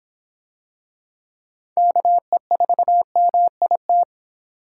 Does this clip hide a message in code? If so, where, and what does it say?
Yes: Morse code from 1.8 to 4.0 s, heard as a keyed tone, about 26 wpm, 700 Hz. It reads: KE4MIT